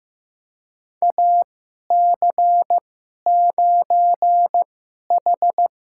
Morse code, heard as a keyed tone, about 15 wpm, spaced as for 11 wpm, 700 Hz. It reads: AC9H
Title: USA Call Signs - 2x Character Spacing - 15wpm